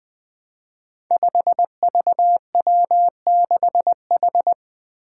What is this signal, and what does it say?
Morse code, heard as a keyed tone, about 20 wpm, 700 Hz. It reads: 5VW6H
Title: Five Challenging Letter-Number Combinations 20wpm